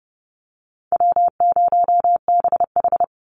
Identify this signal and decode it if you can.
Morse code, heard as a keyed tone, about 30 wpm, 700 Hz. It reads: W0BH
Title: Call Signs - 30wpm